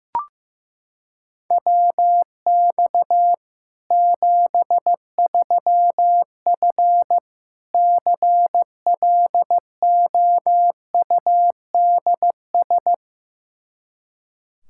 Morse code, heard as a keyed tone, about 15 wpm, 700 Hz. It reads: WX 73F CLOUDS
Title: Practice QSO - Exchange 2 - Info - 15wpm